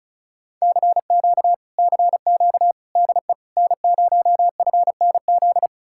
Morse code, heard as a keyed tone, about 48 wpm, spaced as for 42 wpm, 700 Hz. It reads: CQ CQ DE N0FNZ
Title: Speed-Racing Common QSO Phrases Encore - 35wpm